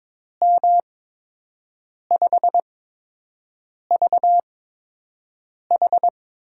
Morse code, heard as a keyed tone, about 22 wpm, spaced as for 6 wpm, 700 Hz. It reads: M5VH
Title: USA Call Signs - 8x Character Spacing - 22wpm